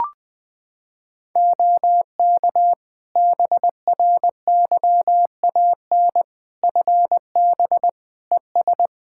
Morse code, heard as a keyed tone, about 20 wpm, 700 Hz. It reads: OK BRYAN FB ES
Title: Practice QSO - Exchange 2 - Info - 20wpm